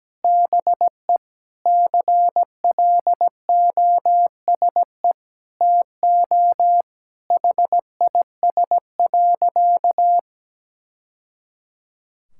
Morse code, heard as a keyed tone, about 17 wpm, 700 Hz. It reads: BE CLOSE TO HIS.
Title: Sentences from Top 1000 Words - Review Part 6 - 17wpm